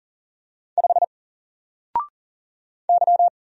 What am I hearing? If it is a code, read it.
Morse code, heard as a keyed tone, about 40 wpm, 700 Hz. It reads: 5 Y
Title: Single Letter-Number Alt 40wpm